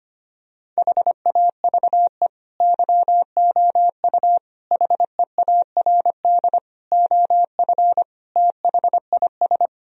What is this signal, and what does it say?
Morse code, heard as a keyed tone, about 25 wpm, 700 Hz. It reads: HAVE YOU HEARD OF THIS
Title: Sentences from Top 1000 Words - Review Part 3 - 25wpm